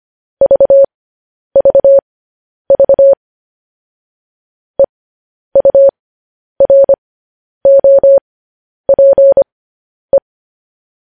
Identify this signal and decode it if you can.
Morse code, heard as a keyed tone, about 25 wpm, spaced as for 10 wpm, 550 Hz. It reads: VVV EUROPE
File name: podcast / News headlines-1x-25x10